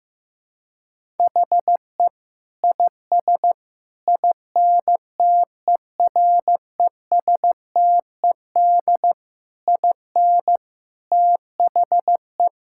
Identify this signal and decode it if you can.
Morse code, heard as a keyed tone, about 14 wpm, 700 Hz. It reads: HE IS INTERESTED IN THE
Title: Sentences from Top 1000 Words - Review Part 6 - 15wpm